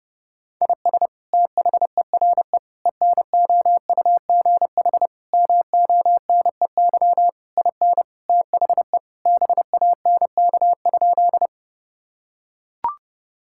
Morse code, heard as a keyed tone, about 30 wpm, 700 Hz. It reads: IS THERE ENOUGH MONEY IN THE BANK?
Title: Sentences from Top 1000 Words - Review Part 6 - 30wpm